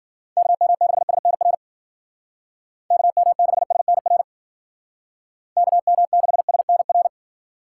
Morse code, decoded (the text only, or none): KM6SNR KM6SNR KM6SNR